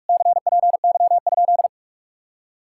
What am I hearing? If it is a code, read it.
Morse code, heard as a keyed tone, about 58 wpm, 700 Hz. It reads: CPY?